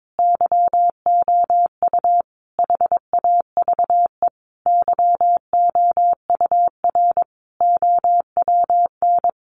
Morse code, heard as a keyed tone, about 22 wpm, 700 Hz. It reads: YOU HAVE YOUR OWN
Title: Sentences from Top 1200 Words 22wpm